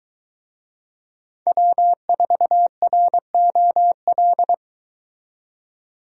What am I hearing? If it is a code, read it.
Morse code, heard as a keyed tone, about 23 wpm, 700 Hz. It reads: W4ROL